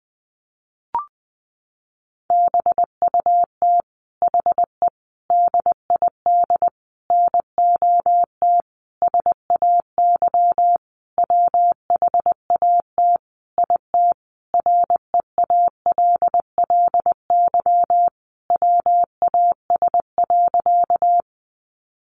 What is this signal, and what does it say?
Morse code, heard as a keyed tone, about 20 wpm, 700 Hz. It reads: BUT HE DID NOT SAY WHAT IT REALLY WAS.
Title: Sentences from Top 600 Words 20wpm